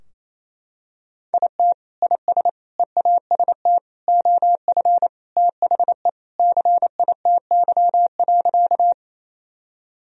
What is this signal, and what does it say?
Morse code, heard as a keyed tone, about 28 wpm, 700 Hz. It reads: IT IS EAST OF THE CITY.
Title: Sentences from Top 1000 Words - Review Part 3 - 28wpm